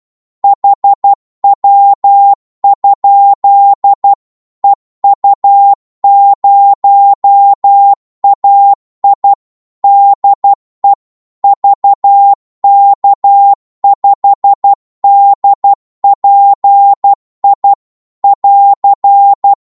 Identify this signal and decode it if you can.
Morse code, heard as a keyed tone, about 12 wpm, 800 Hz. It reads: HW? EU0AI DE VK5DPI <AR>
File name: Morse / qso-12wpm